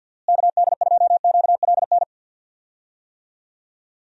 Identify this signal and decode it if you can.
Morse code, heard as a keyed tone, about 49 wpm, 700 Hz. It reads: KD2QLN
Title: USA Call Signs - 6 Characters - Repeated 2x - 50wpm